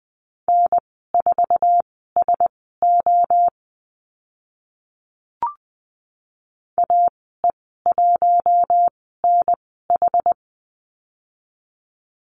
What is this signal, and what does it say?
Morse code, heard as a keyed tone, about 20 wpm, spaced as for 14 wpm, 700 Hz. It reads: N4SO AE1NH
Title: 069 Lesson 12g - Letter d - Call Signs 20wpm